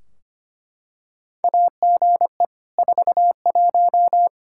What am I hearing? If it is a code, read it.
Morse code, heard as a keyed tone, about 25 wpm, 700 Hz. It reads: AGE 41
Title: Practice QSO - Exchange 3 - More Info - 25wpm